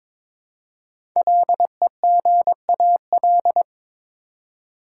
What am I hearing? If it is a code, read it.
Morse code, heard as a keyed tone, about 22 wpm, 700 Hz. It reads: LEGAL
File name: 5 Letter Words - First 3 Letters Given - 22wpm